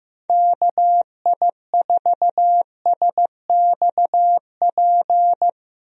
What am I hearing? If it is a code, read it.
Morse code, heard as a keyed tone, about 15 wpm, 700 Hz. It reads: KI4SXP
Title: USA Call Signs - 6 Characters - Repeated 3x - 15wpm